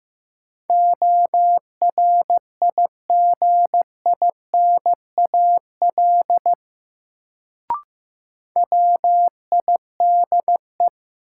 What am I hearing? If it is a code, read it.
Morse code, heard as a keyed tone, about 16 wpm, 700 Hz. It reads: ORIGINAL WIDE
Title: Sentences from Top 900 Words - New Word Drill - 15wpm